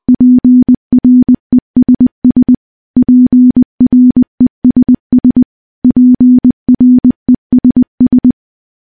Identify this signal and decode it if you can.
Morse code, heard as a keyed tone, about 20 wpm, 250 Hz. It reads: PRESS PRESS PRESS
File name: morse / morse-training-20WPM